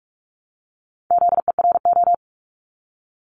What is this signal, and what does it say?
Morse code, heard as a keyed tone, about 46 wpm, 700 Hz. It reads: ZERO
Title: Multi-letter Phonemes Repeated 3x 45wpm